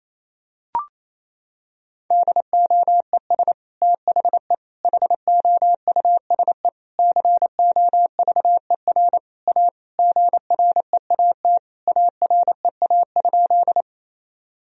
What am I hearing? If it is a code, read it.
Morse code, heard as a keyed tone, about 28 wpm, 700 Hz. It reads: DOES THE HOUSE COVER A GREAT AREA?
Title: Sentences from Top 1000 Words - Review Part 2 - 28wpm